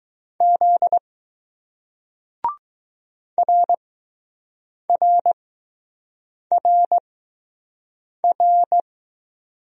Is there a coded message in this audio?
Yes: Morse code from 0.4 to 8.8 s, heard as a keyed tone, about 20 wpm, spaced as for 6 wpm, 700 Hz. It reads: Z RRRR